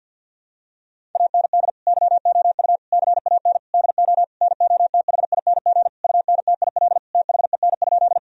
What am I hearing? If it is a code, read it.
Morse code, heard as a keyed tone, about 48 wpm, 700 Hz. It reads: AND YOU CAN DO NOTHING UNTIL THEN?
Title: Sentences from Top 1000 Words - Review Part 1 - 50wpm